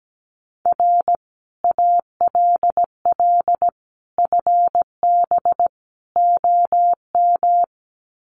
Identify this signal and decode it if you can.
Morse code, heard as a keyed tone, about 17 wpm, 700 Hz. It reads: R ALL FB OM
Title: Common QSO Phrases 17wpm